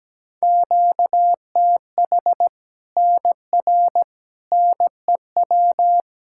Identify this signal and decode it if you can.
Morse code, heard as a keyed tone, about 17 wpm, 700 Hz. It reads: QTH NR NEW
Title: Practice QSO - Exchange 1 - Beginning - 17wpm